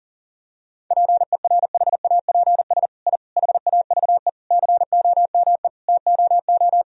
Morse code, heard as a keyed tone, about 40 wpm, 700 Hz. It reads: PERHAPS I HAVE COME TOO
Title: Sentences from Top 1000 Words - Review Part 2 - 40wpm